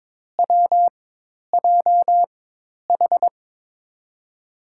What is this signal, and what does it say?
Morse code, heard as a keyed tone, about 21 wpm, spaced as for 10 wpm, 700 Hz. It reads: WJH
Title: USA Call Signs - 6 Characters - 4x Character Spacing - 22wpm